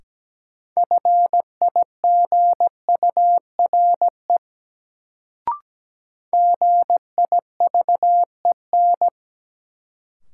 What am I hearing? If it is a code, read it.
Morse code, heard as a keyed tone, about 17 wpm, 700 Hz. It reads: FIGURE GIVEN
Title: Sentences from Top 1400 Words - New Word Drill Alt - 17wpm